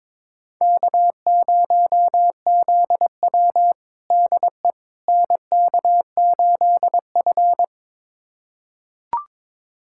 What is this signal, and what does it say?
Morse code, heard as a keyed tone, about 22 wpm, 700 Hz. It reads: K0ZW DE NK8F
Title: USA Callsign DE Callsign - 4 Characters - 22wpm